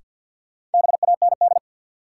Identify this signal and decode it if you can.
Morse code, heard as a keyed tone, about 51 wpm, 700 Hz. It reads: BAND
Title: Sentences from Top 1600 Words - New Word Drill - 50wpm